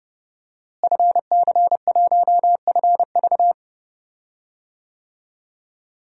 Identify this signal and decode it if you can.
Morse code, heard as a keyed tone, about 32 wpm, 700 Hz. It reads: FC1FV
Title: DXCC-100 Call Signs - 5 Characters - Repeated 3x - 30wpm